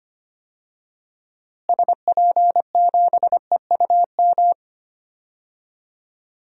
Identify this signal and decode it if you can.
Morse code, heard as a keyed tone, about 25 wpm, 700 Hz. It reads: SP7EUM
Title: DXCC-100 Call Signs - 6 Characters - Repeated 2x - 25wpm